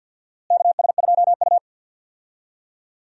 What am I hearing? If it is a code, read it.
Morse code, heard as a keyed tone, about 51 wpm, 700 Hz. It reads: KS2U
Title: USA Call Signs - 4 Characters - Repeated 2x - 50wpm